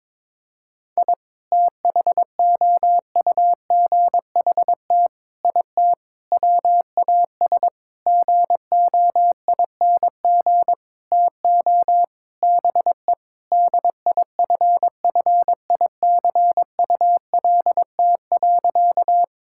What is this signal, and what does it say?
Morse code, heard as a keyed tone, about 22 wpm, 700 Hz. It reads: I THOUGHT IT WAS GOING TO BE DIFFICULT.
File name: Sentences with 7-Word Phrases - 22wpm